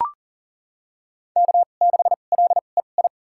Morse code, heard as a keyed tone, about 40 wpm, spaced as for 29 wpm, 700 Hz. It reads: K6LEI
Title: USA Call Signs - 2x Character Spacing - 40wpm